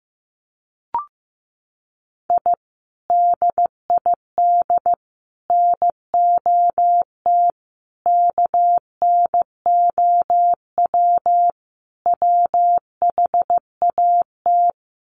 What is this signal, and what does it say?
Morse code, heard as a keyed tone, about 15 wpm, 700 Hz. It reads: I DID NOT KNOW WHAT
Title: Sets of 5 Words 15wpm